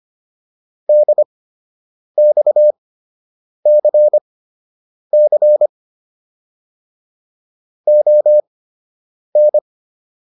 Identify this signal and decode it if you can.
Morse code, heard as a keyed tone, about 25 wpm, spaced as for 8 wpm, 600 Hz. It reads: DXCC ON